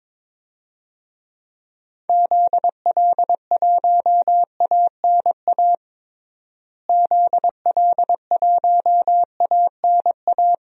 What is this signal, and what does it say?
Morse code, heard as a keyed tone, about 22 wpm, 700 Hz. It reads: ZL1ANA ZL1ANA
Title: DXCC-100 Call Signs - 6 Characters - Repeated 3x - 22wpm